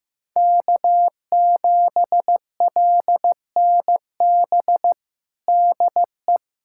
Morse code, heard as a keyed tone, about 15 wpm, 700 Hz. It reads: K7LNB DE
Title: USA Callsign DE Callsign 15wpm